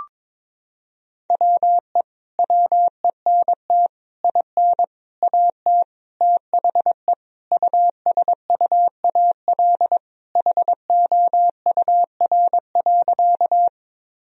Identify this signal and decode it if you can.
Morse code, heard as a keyed tone, about 21 wpm, 700 Hz. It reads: WE WENT IN AT THE USUAL HOUR.
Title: Sentences from Top 1000 Words - Review Part 2 - 22wpm